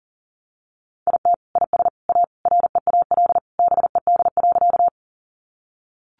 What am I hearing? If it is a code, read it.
Morse code, heard as a keyed tone, about 40 wpm, 700 Hz. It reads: IT IS A REAL BED.
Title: Sentences from Top 500 Words 40wpm